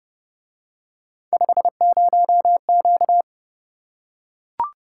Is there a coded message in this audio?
Yes: Morse code from 1.3 to 3.2 s, heard as a keyed tone, about 29 wpm, 700 Hz. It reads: 50Q